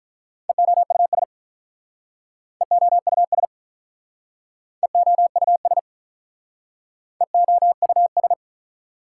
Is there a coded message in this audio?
Yes: Morse code from 0.5 to 8.3 s, heard as a keyed tone, about 46 wpm, 700 Hz. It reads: EOUS EOUS EOUS EOUS